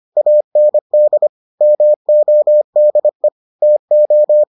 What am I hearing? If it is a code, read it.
Morse code, heard as a keyed tone, about 25 wpm, 600 Hz. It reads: AND MODE TO